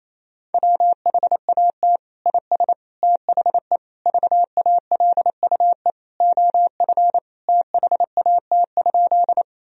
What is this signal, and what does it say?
Morse code, heard as a keyed tone, about 28 wpm, 700 Hz. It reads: WHAT IS THE VALUE OF THAT?
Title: Sentences from Top 1000 Words - Review Part 3 - 28wpm